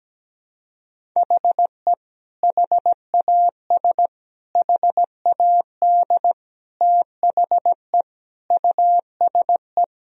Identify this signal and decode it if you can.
Morse code, heard as a keyed tone, about 17 wpm, 700 Hz. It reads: HE HAS HAD THE USE
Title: Sentences from Top 1000 Words - Review Part 1 - 17wpm